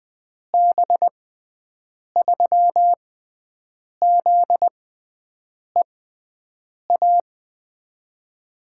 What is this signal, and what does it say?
Morse code, heard as a keyed tone, about 20 wpm, spaced as for 7 wpm, 700 Hz. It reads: B3ZEA
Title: USA Call Signs - 6 Characters - 6x Character Spacing - 20wpm